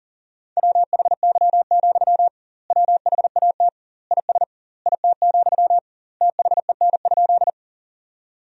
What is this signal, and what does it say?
Morse code, heard as a keyed tone, about 40 wpm, 700 Hz. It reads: WHY, WHAT IS IT, THEN?
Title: 2x Word-Spacing - Sentences from Top 200 Words - 40wpm